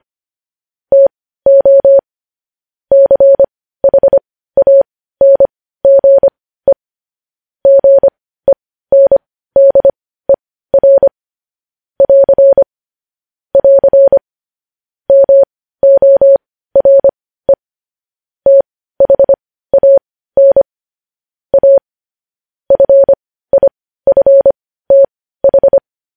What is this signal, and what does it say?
Morse code, heard as a keyed tone, about 25 wpm, spaced as for 15 wpm, 550 Hz. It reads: TO CHANGE GENDER <AR> <AR> MORE THAN A FIFTH
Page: news headlines in morse code at 15 wpm